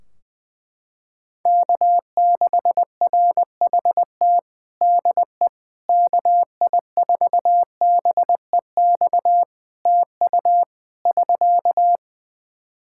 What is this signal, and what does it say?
Morse code, heard as a keyed tone, about 20 wpm, 700 Hz. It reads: K6RHT DE KI4BEX TU <SK>
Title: K1USN SST QSO Practice 20wpm